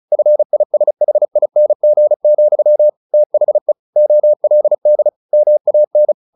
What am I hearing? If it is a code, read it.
Morse code, heard as a keyed tone, about 34 wpm, 600 Hz. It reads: FISHING, THE OLD MAN
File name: the old man and the sea / the old man and the sea 1 35wpm